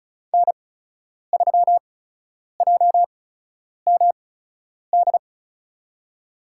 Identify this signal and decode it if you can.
Morse code, heard as a keyed tone, about 34 wpm, spaced as for 10 wpm, 700 Hz. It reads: N3JMD
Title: USA Call Signs - 5 Characters - 8x Character Spacing - 35wpm